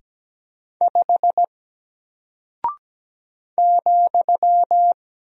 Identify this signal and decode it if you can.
Morse code, heard as a keyed tone, about 17 wpm, 700 Hz. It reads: 5 ,
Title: Single Character (Letter:Number:Punctuation:Prosign) Alt 17wpm